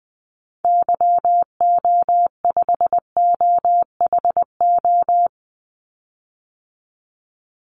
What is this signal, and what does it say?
Morse code, heard as a keyed tone, about 20 wpm, 700 Hz. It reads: YO5OHO
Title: Call Signs Repeated 3x - 20wpm